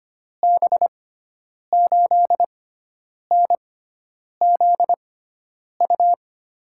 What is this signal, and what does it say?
Morse code, heard as a keyed tone, about 25 wpm, spaced as for 9 wpm, 700 Hz. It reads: B8NZU
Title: USA Call Signs - 6 Characters - 6x Character Spacing - 25wpm